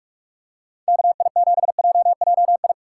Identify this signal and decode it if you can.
Morse code, heard as a keyed tone, about 43 wpm, 700 Hz. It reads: KI7JJI
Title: USA Call Signs - 6x Character Spacing - 45wpm